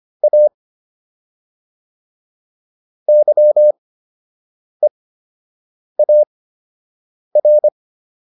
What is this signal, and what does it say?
Morse code, heard as a keyed tone, about 25 wpm, spaced as for 7 wpm, 600 Hz. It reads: A YEAR